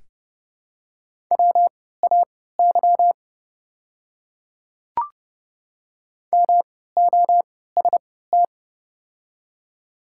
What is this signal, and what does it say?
Morse code, heard as a keyed tone, about 30 wpm, spaced as for 17 wpm, 700 Hz. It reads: WAY MOST
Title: 3x Character Spacing - Top 100 Words - 30wpm